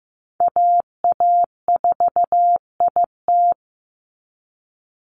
Morse code, heard as a keyed tone, about 15 wpm, 700 Hz. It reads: AA4IT
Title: NA SOTA Chaser Callsigns 15wpm